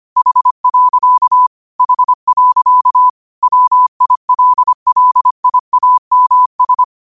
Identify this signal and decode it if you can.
Morse code, heard as a keyed tone, about 25 wpm, 1000 Hz. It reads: H. H. WILLIAMS